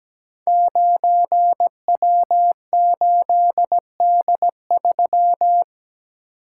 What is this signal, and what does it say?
Morse code, heard as a keyed tone, about 17 wpm, 700 Hz. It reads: 9W8D3
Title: Code Groups - 5 Letter:Numbers - 17wpm